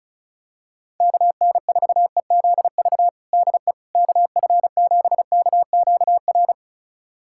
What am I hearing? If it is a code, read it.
Morse code, heard as a keyed tone, about 36 wpm, 690 Hz. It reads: KN4EZV DE KF7KQR
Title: USA Callsign DE Callsign 35wpm